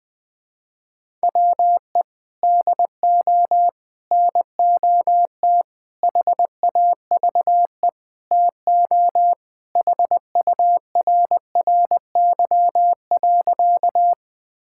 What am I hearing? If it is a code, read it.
Morse code, heard as a keyed tone, about 21 wpm, 700 Hz. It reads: WE DO NOT HAVE TO HURRY.
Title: Sentences with 5-Word Phrases - 20wpm